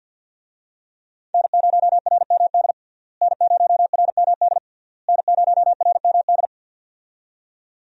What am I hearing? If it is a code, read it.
Morse code, heard as a keyed tone, about 47 wpm, 700 Hz. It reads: N0RMD N0RMD N0RMD
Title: USA Call Signs - 5 Characters - Repeated 3x - 50wpm